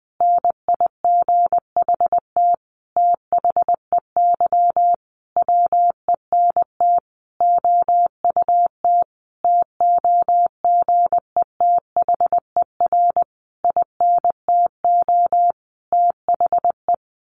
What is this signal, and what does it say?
Morse code, heard as a keyed tone, about 20 wpm, 700 Hz. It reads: NIGHT THEY WENT OUT TOGETHER INTO THE
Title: Sentences from Top 600 Words 20wpm